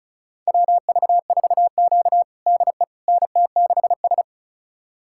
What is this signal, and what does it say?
Morse code, heard as a keyed tone, about 36 wpm, 700 Hz. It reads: WV4Q DE NT6S
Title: USA Callsign DE Callsign - 4 Characters - 35wpm